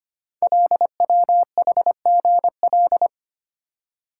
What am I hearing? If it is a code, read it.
Morse code, heard as a keyed tone, about 25 wpm, 700 Hz. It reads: LWHGL
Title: Code Groups - 5 Letter:Numbers - 25wpm